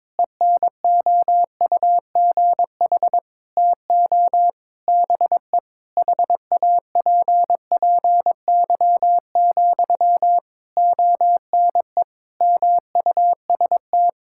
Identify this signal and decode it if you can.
Morse code, heard as a keyed tone, about 22 wpm, 700 Hz. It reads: ENOUGH TO BE HAPPY, ONE MUST
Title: Sentences from Top 1000 Words - Review Part 6 - 22wpm